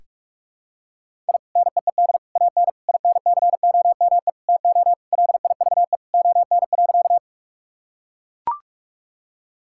Morse code, heard as a keyed tone, about 42 wpm, 700 Hz. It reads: I NEED AN INCOME TO LIVE ON.
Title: Sentences from Top 1000 Words - Review Part 5 - 45wpm